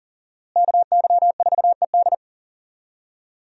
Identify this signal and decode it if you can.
Morse code, heard as a keyed tone, about 40 wpm, 700 Hz. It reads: KY4ED
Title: USA Call Signs - 5 Characters - Repeated 2x - 40wpm